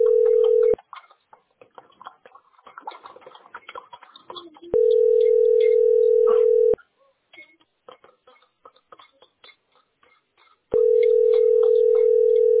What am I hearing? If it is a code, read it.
Ringback tone (North America), 440 + 480 Hz.